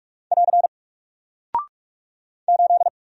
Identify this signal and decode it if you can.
Morse code, heard as a keyed tone, about 44 wpm, 700 Hz. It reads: <AR> 8
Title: Single Character (Letter:Number:Punctuation:Prosign) Alt 45wpm